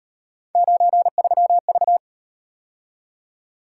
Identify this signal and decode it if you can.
Morse code, heard as a keyed tone, about 38 wpm, 700 Hz. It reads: 93V